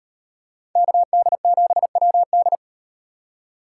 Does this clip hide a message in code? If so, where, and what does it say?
Yes: Morse code from 0.8 to 2.6 s, heard as a keyed tone, about 38 wpm, 700 Hz. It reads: KD7WD